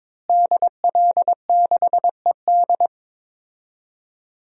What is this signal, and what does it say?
Morse code, heard as a keyed tone, about 22 wpm, 700 Hz. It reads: DL6ED
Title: DXCC-100 Call Signs - 5 Characters - Repeated 2x - 22wpm